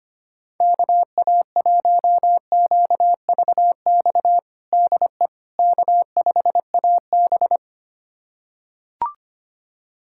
Morse code, heard as a keyed tone, about 25 wpm, 700 Hz. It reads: KA1QVX DE K5AB
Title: USA Callsign DE Callsign 25wpm